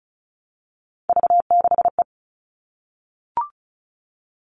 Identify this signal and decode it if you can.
Morse code, heard as a keyed tone, about 34 wpm, 700 Hz. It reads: V6E
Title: All Cut Numbers - 3-Digit 35wpm